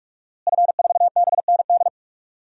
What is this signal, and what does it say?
Morse code, heard as a keyed tone, about 44 wpm, 700 Hz. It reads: F4BND
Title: DXCC-100 Call Signs - 5 Characters - Repeated 3x - 45wpm